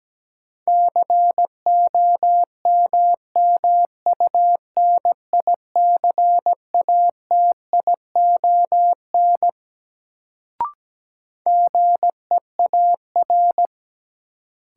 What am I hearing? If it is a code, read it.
Morse code, heard as a keyed tone, about 17 wpm, 700 Hz. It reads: COMMUNICATION GEAR
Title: Sentences from Top 1400 Words - New Word Drill Alt - 17wpm